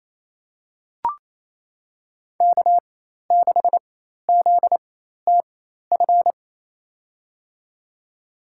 Morse code, heard as a keyed tone, about 28 wpm, spaced as for 13 wpm, 700 Hz. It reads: K6ZTF
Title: USA Call Signs - 5 Characters - 4x Character Spacing - 28wpm